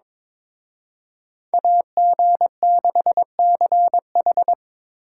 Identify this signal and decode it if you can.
Morse code, heard as a keyed tone, about 26 wpm, 700 Hz. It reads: AG6CH
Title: Speed Racing - USA Call Signs 22wpm